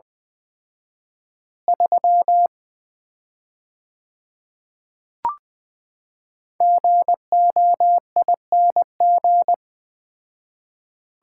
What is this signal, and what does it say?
Morse code, heard as a keyed tone, about 20 wpm, 700 Hz. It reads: 3 GOING
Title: Common QSO Phrases - 4x Word-Spacing - 20wpm